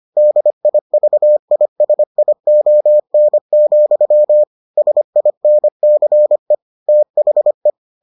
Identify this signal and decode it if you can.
Morse code, heard as a keyed tone, about 25 wpm, 600 Hz. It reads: DIVISION, SINCE THE